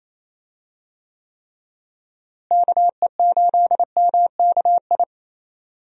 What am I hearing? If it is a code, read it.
Morse code, heard as a keyed tone, about 28 wpm, 700 Hz. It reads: KE8MKI
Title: USA Call Signs - 6 Characters - Repeated 3x - 28wpm